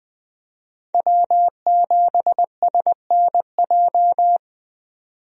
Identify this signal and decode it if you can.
Morse code, heard as a keyed tone, about 20 wpm, 700 Hz. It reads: W7SNJ
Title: USA Call Signs - 5 Characters - 20wpm